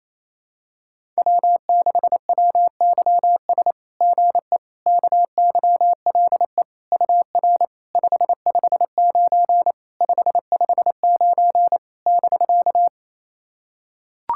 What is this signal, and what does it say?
Morse code, heard as a keyed tone, about 28 wpm, 690 Hz. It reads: W6WYS GE KYLE UR 559 559 <BK>
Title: SOTA QSO Practice 28wpm